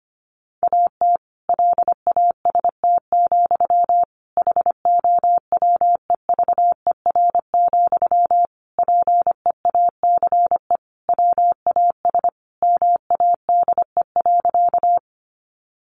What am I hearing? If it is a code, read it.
Morse code, heard as a keyed tone, about 25 wpm, 700 Hz. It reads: AT LAST, HOWEVER, PEACE WAS MADE.
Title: Sentences from Top 1500 Words 25wpm